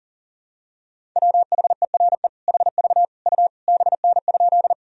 Morse code, heard as a keyed tone, about 40 wpm, 700 Hz. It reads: WHERE HV U BN?
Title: Sentences from Top 200 Words and Abbreviations 40wpm